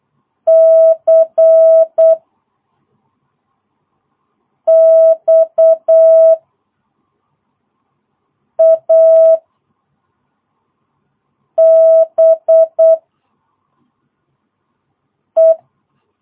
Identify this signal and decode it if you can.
Morse code, heard as a keyed tone, about 8 wpm, 660 Hz. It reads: C X A B E
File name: jh8cbh / 2022cw